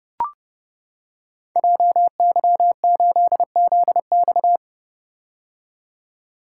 Morse code, heard as a keyed tone, about 30 wpm, 700 Hz. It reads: JY8ZX
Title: Code Groups - 5 Letter:Numbers - 30wpm